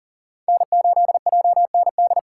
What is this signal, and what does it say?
Morse code, heard as a keyed tone, about 40 wpm, 700 Hz. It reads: N8JND
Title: USA Call Signs - 4x Character Spacing - 40wpm